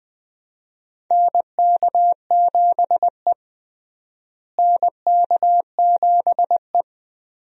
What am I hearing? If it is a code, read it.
Morse code, heard as a keyed tone, about 20 wpm, 700 Hz. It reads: NK7E NK7E
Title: USA Call Signs - 4 Characters - Repeated 2x - 20wpm